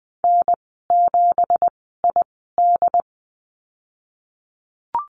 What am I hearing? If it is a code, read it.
Morse code, heard as a keyed tone, about 20 wpm, spaced as for 15 wpm, 700 Hz. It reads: N7ID
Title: 207 Lesson 33g - Letter j - Call Signs 20wpm